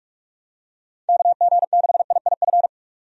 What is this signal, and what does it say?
Morse code, heard as a keyed tone, about 46 wpm, 700 Hz. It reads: KG6IIF
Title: USA Call Signs - 6 Characters - 45wpm